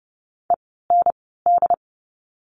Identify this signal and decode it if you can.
Morse code, heard as a keyed tone, about 30 wpm, spaced as for 17 wpm, 700 Hz. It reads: END
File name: Lesson 17f - Letter u - Words 30wpm